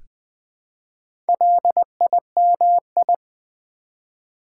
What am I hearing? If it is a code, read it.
Morse code, heard as a keyed tone, about 20 wpm, 700 Hz. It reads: LIMI
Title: Build A Word - 1 Letter At A Time - 3 to 5-Letter Words - 20wpm